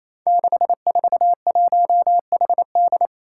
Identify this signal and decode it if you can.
Morse code, heard as a keyed tone, about 28 wpm, 700 Hz. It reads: 641HD
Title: Five Challenging Letter-Number Combinations 28wpm